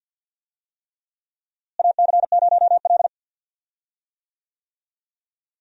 Morse code, heard as a keyed tone, about 50 wpm, 700 Hz. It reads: AC0L
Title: USA Call Signs - 4 Characters - Repeated 3x - 50wpm